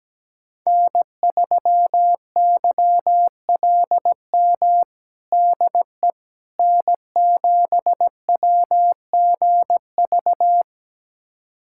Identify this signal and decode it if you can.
Morse code, heard as a keyed tone, about 17 wpm, 700 Hz. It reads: N3YLM DE N7WGV
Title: USA Callsign DE Callsign - 5 Characters - 17wpm